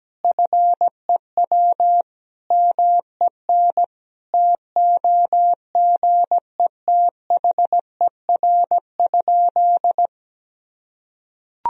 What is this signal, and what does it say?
Morse code, heard as a keyed tone, about 17 wpm, 700 Hz. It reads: FEW MEN TOGETHER?
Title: Sentences from Top 1000 Words - Review Part 1 - 17wpm